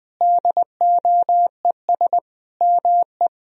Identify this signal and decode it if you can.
Morse code, heard as a keyed tone, about 20 wpm, 700 Hz. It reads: DOES ME
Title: Sentences from Top 200 Words and Abbreviations 20wpm